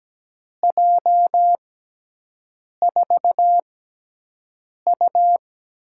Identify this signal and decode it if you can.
Morse code, heard as a keyed tone, about 17 wpm, spaced as for 6 wpm, 700 Hz. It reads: J4U
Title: USA Call Signs - 6 Characters - 6x Character Spacing - 17wpm